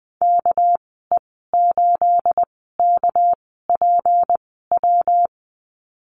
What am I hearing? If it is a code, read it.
Morse code, heard as a keyed tone, about 20 wpm, spaced as for 15 wpm, 700 Hz. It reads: KE8KPW
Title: 219 Lesson 35g - Number 0 - Call Signs 20wpm